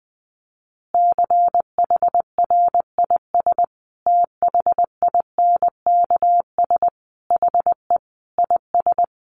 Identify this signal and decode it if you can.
Morse code, heard as a keyed tone, about 20 wpm, 700 Hz. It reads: CHRIS THINKS HE IS
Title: Additional Easy Phrases 20wpm